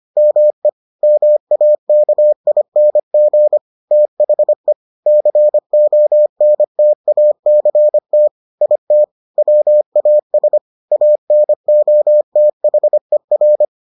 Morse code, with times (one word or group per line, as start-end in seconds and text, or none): 0.2-0.7 ME
1.0-3.6 MAKING
3.9-4.7 THE
5.1-8.3 CONTACT
8.6-9.0 IT
9.4-10.6 WAS
10.9-13.6 ANOTHER